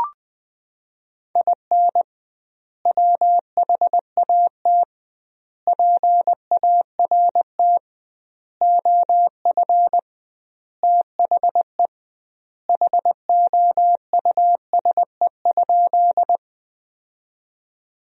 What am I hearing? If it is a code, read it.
Morse code, heard as a keyed tone, about 20 wpm, 700 Hz. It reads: IN WHAT PART OF THE HOUSE?
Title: 2x Word-Spacing - Sentences from Top 200 Words - 20wpm